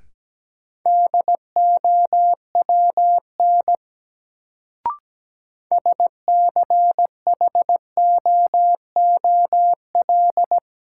Morse code, heard as a keyed tone, about 17 wpm, 700 Hz. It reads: DOWN SCHOOL
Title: Sets of 2 Words - New Word Drill - 17wpm